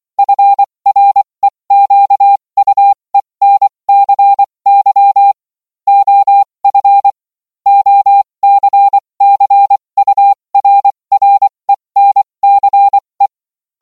Morse code, heard as a keyed tone, about 24 wpm, spaced as for 20 wpm, 800 Hz. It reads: FREQUENCY OF OCCURRENCE